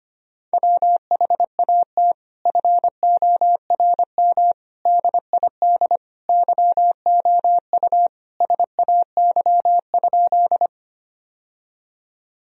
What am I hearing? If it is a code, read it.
Morse code, heard as a keyed tone, about 25 wpm, 700 Hz. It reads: WHAT FORM DID YOU SAY?
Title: Sentences from Top 1000 Words - Review Part 1 - 25wpm